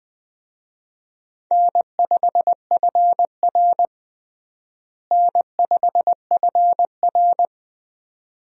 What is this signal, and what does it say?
Morse code, heard as a keyed tone, about 20 wpm, 700 Hz. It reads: N5FR N5FR
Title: USA Call Signs - 4 Characters - Repeated 2x - 20wpm